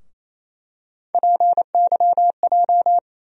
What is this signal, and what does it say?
Morse code, heard as a keyed tone, about 28 wpm, 700 Hz. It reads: PYJ
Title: USA Call Sign Postfixes - 3 Letters 28wpm